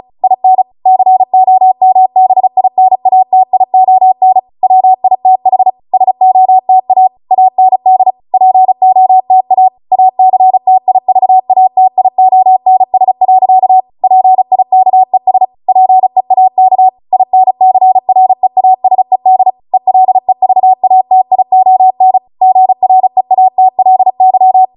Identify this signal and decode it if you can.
Morse code, heard as a keyed tone, about 36 wpm, 750 Hz. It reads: IN COMBINATION WITH SOTA AND POTA ACTIVATIONS. PIKES PEAK INCREASED ELEVATION GREATLY